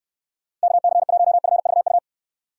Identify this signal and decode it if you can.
Morse code, heard as a keyed tone, about 69 wpm, 700 Hz. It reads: KC1FVU